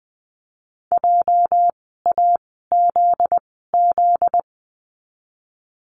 Morse code, heard as a keyed tone, about 20 wpm, spaced as for 15 wpm, 700 Hz. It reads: JAZZ
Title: 244 Lesson 39f - Letter z - Words 20wpm